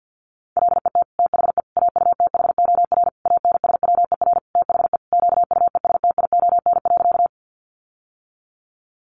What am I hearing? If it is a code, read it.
Morse code, heard as a keyed tone, about 50 wpm, 700 Hz. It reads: LET THE AUTHOR ANSWER THE QUESTION.
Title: Sentences from Top 1200 Words 50wpm